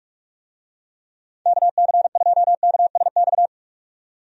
Morse code, heard as a keyed tone, about 42 wpm, 700 Hz. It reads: KC2KSX
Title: USA Call Signs - 6 Characters - 45wpm